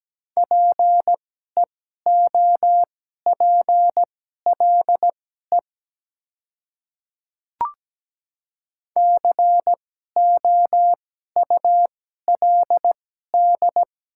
Morse code, heard as a keyed tone, about 17 wpm, spaced as for 12 wpm, 700 Hz. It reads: PEOPLE COULD
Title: 2x Character Spacing - Top 100 Words - 17wpm